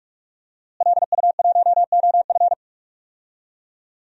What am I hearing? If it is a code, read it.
Morse code, heard as a keyed tone, about 46 wpm, 700 Hz. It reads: LU1OF
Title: DXCC-100 Call Signs - 5 Characters - Repeated 3x - 45wpm